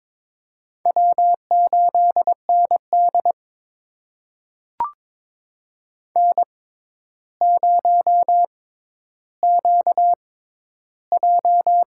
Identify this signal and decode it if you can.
Morse code, heard as a keyed tone, about 22 wpm, spaced as for 11 wpm, 700 Hz. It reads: W8ND N0QJ
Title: USA Call Signs - 4 Characters - 6x Character Spacing - 22wpm